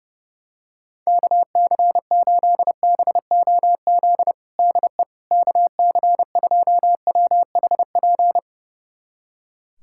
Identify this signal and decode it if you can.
Morse code, heard as a keyed tone, about 32 wpm, 700 Hz. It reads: KC8BOZ DE KC2WHP
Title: USA Callsign DE Callsign - 6 Characters - 30wpm